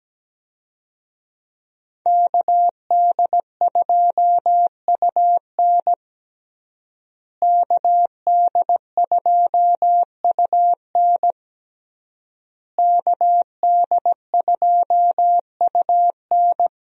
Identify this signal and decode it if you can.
Morse code, heard as a keyed tone, about 17 wpm, 700 Hz. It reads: KD2UN KD2UN KD2UN
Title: USA Call Signs - 5 Characters - Repeated 3x - 17wpm